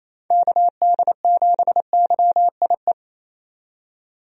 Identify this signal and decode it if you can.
Morse code, heard as a keyed tone, about 28 wpm, 700 Hz. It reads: KD7YIE
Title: USA Call Signs - 6 Characters - 28wpm